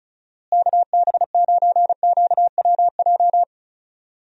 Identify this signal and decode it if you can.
Morse code, heard as a keyed tone, about 36 wpm, 700 Hz. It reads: KB9QWJ
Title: USA Call Signs - 6 Characters - Repeated 2x - 35wpm